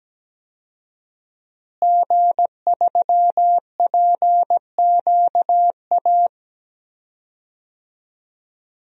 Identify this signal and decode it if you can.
Morse code, heard as a keyed tone, about 17 wpm, 700 Hz. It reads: G3PQA
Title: DXCC-100 Call Signs - 5 Characters - Repeated 2x - 17wpm